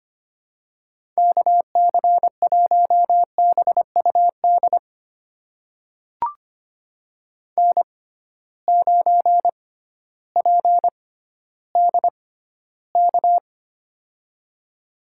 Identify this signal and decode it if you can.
Morse code, heard as a keyed tone, about 25 wpm, spaced as for 14 wpm, 700 Hz. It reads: KC1BUD N9PDK
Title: USA Call Signs - 6x Character Spacing - 25wpm